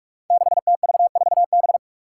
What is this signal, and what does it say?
Morse code, heard as a keyed tone, about 46 wpm, 700 Hz. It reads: 6TV4B
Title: Five Challenging Letter-Number Combinations 45wpm